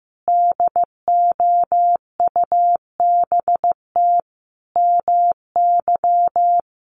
Morse code, heard as a keyed tone, about 15 wpm, 700 Hz. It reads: DOUBT MY
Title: Sentences from Top 1500 Words 15wpm